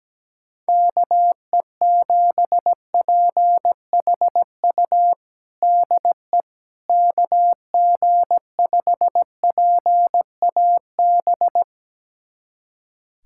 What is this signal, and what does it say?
Morse code, heard as a keyed tone, about 17 wpm, 700 Hz. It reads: KE7PHU DE KG5PAB
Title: USA Callsign DE Callsign - 6 Characters - 17wpm